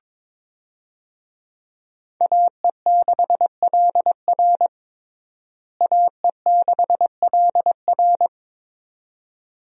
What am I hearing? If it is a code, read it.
Morse code, heard as a keyed tone, about 22 wpm, 700 Hz. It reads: AE6LR AE6LR
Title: USA Call Signs - 5 Characters - Repeated 2x - 22wpm